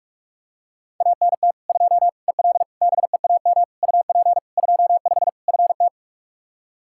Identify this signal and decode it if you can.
Morse code, heard as a keyed tone, about 46 wpm, 700 Hz. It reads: ANT 2 EL BEAM UP 25 FT
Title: Practice QSO - Exchange 2 - Info - 45wpm